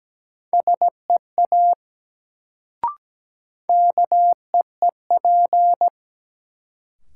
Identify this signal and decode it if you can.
Morse code, heard as a keyed tone, about 17 wpm, 700 Hz. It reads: SEA KEEP
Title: Sentences from Top 300 Words - New Word Drill - 17wpm